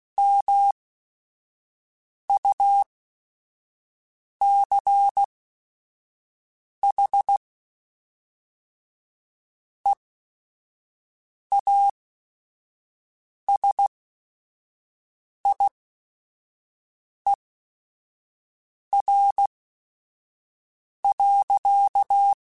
Morse code, heard as a keyed tone, about 16 wpm, spaced as for 5 wpm, 800 Hz. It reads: MUCH EASIER.
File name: Technical Subjects / Technical Subjects-mp3s / AND Gate @5